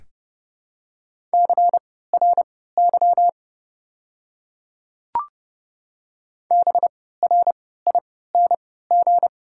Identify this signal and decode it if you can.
Morse code, heard as a keyed tone, about 30 wpm, spaced as for 17 wpm, 700 Hz. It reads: CRY BRING
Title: 3x Character Spacing - Top 500 Words - 30wpm